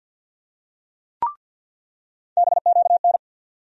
Morse code, heard as a keyed tone, about 53 wpm, 700 Hz. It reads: BQN
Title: USA Call Sign Postfixes - 3 Letters 50wpm